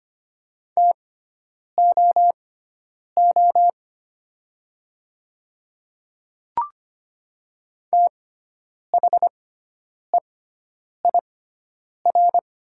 Morse code, heard as a keyed tone, about 25 wpm, spaced as for 9 wpm, 700 Hz. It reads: TOO THEIR
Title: 6x Character Spacing - Top 200 Words - 25wpm